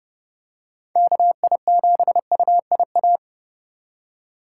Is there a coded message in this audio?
Yes: Morse code from 1.0 to 3.2 s, heard as a keyed tone, about 28 wpm, 700 Hz. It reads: KI7UIA